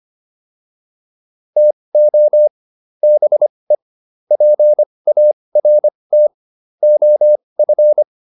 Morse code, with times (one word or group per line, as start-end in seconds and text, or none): 1.6-2.5 TO
3.0-3.8 BE
4.3-6.3 PART
6.8-8.0 OF